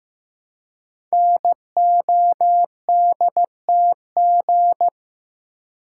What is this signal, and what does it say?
Morse code, heard as a keyed tone, about 15 wpm, 700 Hz. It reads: NODTG
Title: Code Groups - 5 Letters - 15wpm